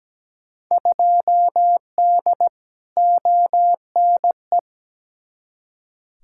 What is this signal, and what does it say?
Morse code, heard as a keyed tone, about 17 wpm, 700 Hz. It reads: 2D ONE
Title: Field Day Exchanges - 17wpm